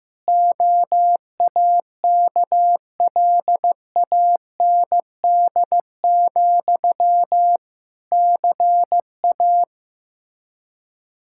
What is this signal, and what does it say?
Morse code, heard as a keyed tone, about 16 wpm, 700 Hz. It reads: OAKLAND, CA
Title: K1USN SST QSO Practice 15wpm